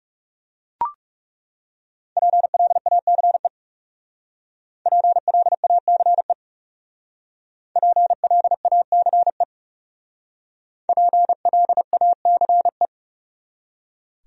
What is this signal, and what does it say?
Morse code, heard as a keyed tone, about 37 wpm, 700 Hz. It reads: PLACE PLACE PLACE PLACE
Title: Speed-Racing Top 200 Words 30wpm